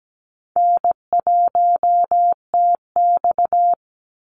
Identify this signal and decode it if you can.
Morse code, heard as a keyed tone, about 17 wpm, 700 Hz. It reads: N1TX
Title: Call Signs Repeated 3x - 17wpm